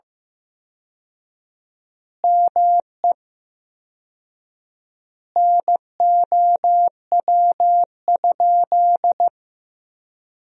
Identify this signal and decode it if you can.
Morse code, heard as a keyed tone, about 15 wpm, 700 Hz. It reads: ME NOW?
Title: 4x Word-Spacing - Sentences from Top 400 Words - 15wpm